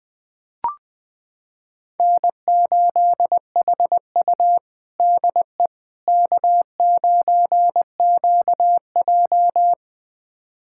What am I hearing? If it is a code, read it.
Morse code, heard as a keyed tone, about 20 wpm, 700 Hz. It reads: N8HU DE K9QJ
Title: USA Callsign DE Callsign - 4 Characters - 20wpm